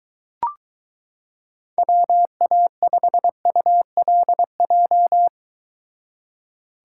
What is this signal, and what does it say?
Morse code, heard as a keyed tone, about 23 wpm, 700 Hz. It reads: WA5ULJ